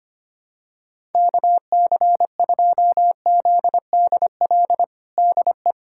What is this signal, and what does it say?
Morse code, heard as a keyed tone, about 25 wpm, 700 Hz. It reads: KC2ZDL DE
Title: USA Callsign DE Callsign 25wpm